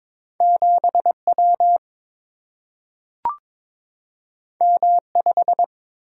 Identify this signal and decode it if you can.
Morse code, heard as a keyed tone, about 22 wpm, 690 Hz. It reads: 7W M5
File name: practice / 22wpm / Two Letter-Number Combinations - 22wpm